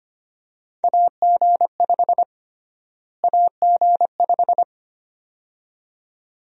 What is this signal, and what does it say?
Morse code, heard as a keyed tone, about 25 wpm, 700 Hz. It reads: AG5 AG5
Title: USA Call Sign Prefixes - Repeated 2x - 25wpm